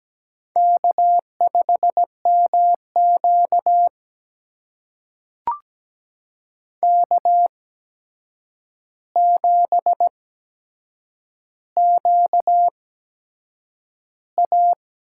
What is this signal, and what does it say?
Morse code, heard as a keyed tone, about 17 wpm, spaced as for 7 wpm, 700 Hz. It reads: K5MQ K7QA
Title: USA Call Signs - 4 Characters - 8x Character Spacing - 17wpm